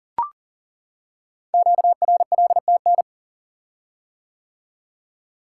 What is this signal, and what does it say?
Morse code, heard as a keyed tone, about 42 wpm, 690 Hz. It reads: QRLTN
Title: Code Groups - 5 Letters - 40wpm